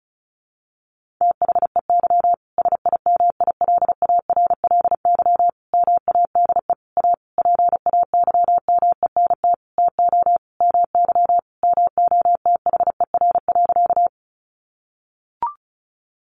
Morse code, heard as a keyed tone, about 34 wpm, 700 Hz. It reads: THEY SIMILARLY MADE A PAYMENT TO MY MOTHER.
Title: Sentences from Top 1400 Words 35wpm